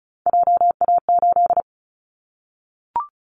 Morse code, heard as a keyed tone, about 36 wpm, 690 Hz. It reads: JA8
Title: Callsign Prefixes for Top 50 DXCC - 35wpm